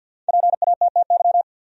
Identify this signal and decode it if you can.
Morse code, heard as a keyed tone, about 50 wpm, 700 Hz. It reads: PATTY
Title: USA Common Names - 50wpm